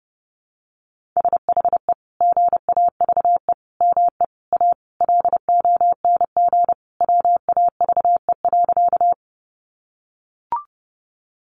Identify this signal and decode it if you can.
Morse code, heard as a keyed tone, about 30 wpm, 700 Hz. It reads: SHE GAVE ME A LONG WAVE.
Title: Sentences from Top 1400 Words 30wpm